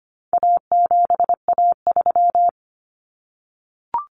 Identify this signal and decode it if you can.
Morse code, heard as a keyed tone, about 25 wpm, 700 Hz. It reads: A7A3
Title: Common Cut Numbers - 4-Digit 25wpm